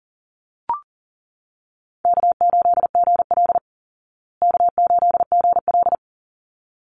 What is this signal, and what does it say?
Morse code, heard as a keyed tone, about 37 wpm, 700 Hz. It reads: K8GL K8GL
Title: Easy Call Signs Repeated 2x - 40wpm